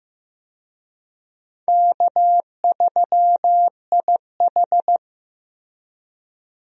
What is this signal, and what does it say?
Morse code, heard as a keyed tone, about 16 wpm, 700 Hz. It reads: K3IH
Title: USA Call Signs - 4 Characters - Repeated 3x - 15wpm